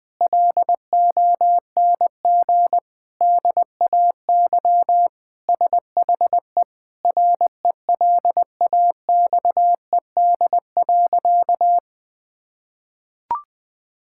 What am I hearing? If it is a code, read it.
Morse code, heard as a keyed tone, about 20 wpm, 700 Hz. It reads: LONG DAY SHE RELAXED.
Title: Sentences with 4-Word Phrases - 20wpm